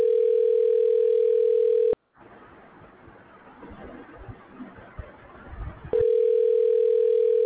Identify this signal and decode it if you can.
Ringback tone (North America), 440 + 480 Hz.